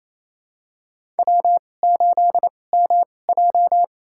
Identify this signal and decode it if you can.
Morse code, heard as a keyed tone, about 28 wpm, spaced as for 20 wpm, 700 Hz. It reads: W8MJ
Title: USA Call Signs - 4 Characters - 2x Character Spacing - 28wpm